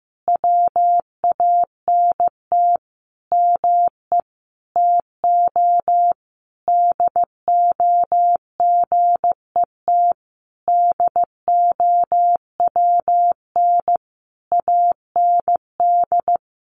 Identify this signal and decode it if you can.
Morse code, heard as a keyed tone, about 15 wpm, 700 Hz. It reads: WANT ME TO DO GET DOWN AND
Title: A Sound Of Thunder 15wpm